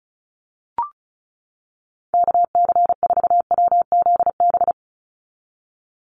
Morse code, heard as a keyed tone, about 36 wpm, 700 Hz. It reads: KC4WZB
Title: NA SOTA Activator Callsigns 35wpm